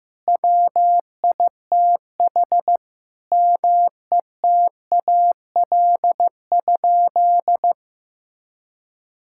Morse code, heard as a keyed tone, about 15 wpm, 700 Hz. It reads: WITH METAL?
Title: Sentences from Top 1600 Words 15wpm